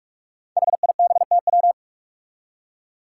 Morse code, heard as a keyed tone, about 46 wpm, 700 Hz. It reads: HIBTW
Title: Five Challenging Letter-Number Combinations 45wpm